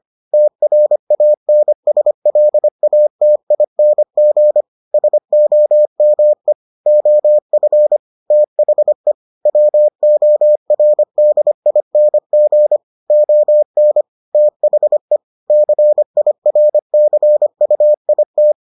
Morse code, heard as a keyed tone, about 25 wpm, 600 Hz. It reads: TRANSLATING SOME OF THE WORDING ON THE CIRCUIT